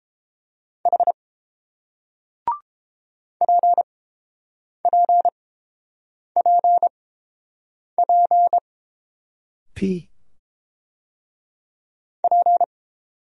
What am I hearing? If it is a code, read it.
Morse code, heard as a keyed tone, about 30 wpm, spaced as for 7 wpm, 700 Hz. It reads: H PPPP P